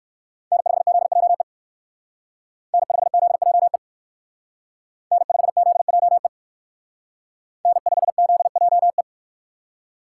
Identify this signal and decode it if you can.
Morse code, heard as a keyed tone, about 53 wpm, 700 Hz. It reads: N5ZJE N5ZJE N5ZJE N5ZJE
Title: Speed Racing - USA 1x3 Call Signs 45wpm